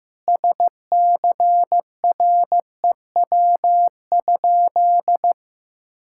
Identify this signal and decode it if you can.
Morse code, heard as a keyed tone, about 15 wpm, 700 Hz. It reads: SCREW?
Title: Sentences from Top 1600 Words 15wpm